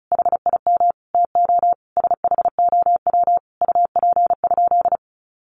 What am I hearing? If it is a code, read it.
Morse code, heard as a keyed tone, about 34 wpm, 700 Hz. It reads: HIM TO SHOW UP?
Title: Sentences from Top 200 Words 35wpm